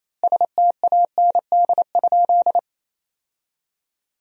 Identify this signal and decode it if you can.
Morse code, heard as a keyed tone, about 28 wpm, 700 Hz. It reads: STAND?
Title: 4x Word-Spacing - Sentences from Top 300 Words - 28wpm